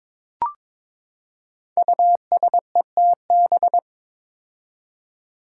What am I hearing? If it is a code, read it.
Morse code, heard as a keyed tone, about 22 wpm, 700 Hz. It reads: USETB
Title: Five Challenging Letter-Number Combinations 22wpm